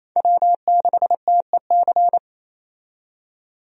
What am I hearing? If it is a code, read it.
Morse code, heard as a keyed tone, about 28 wpm, 700 Hz. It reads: W6TEC
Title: USA Call Signs - 5 Characters - 28wpm